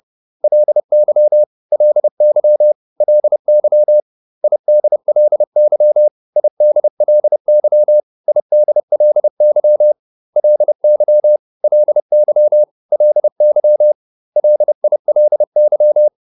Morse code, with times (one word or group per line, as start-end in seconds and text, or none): 0.4-1.4 LY
1.7-2.7 LY
3.0-4.0 LY
4.4-6.1 IDLY
6.4-8.0 IDLY
8.3-9.9 IDLY
10.4-11.4 LY
11.6-12.6 LY
12.9-13.9 LY
14.4-16.1 LILY